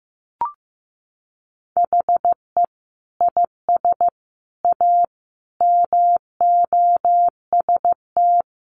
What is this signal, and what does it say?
Morse code, heard as a keyed tone, about 15 wpm, 700 Hz. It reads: HE IS A MOST
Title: Sentences from Top 1000 Words 15wpm